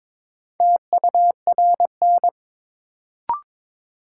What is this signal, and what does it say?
Morse code, heard as a keyed tone, about 22 wpm, 700 Hz. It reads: TURN
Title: Sentences from Top 200 Words - New Word Drill - 22wpm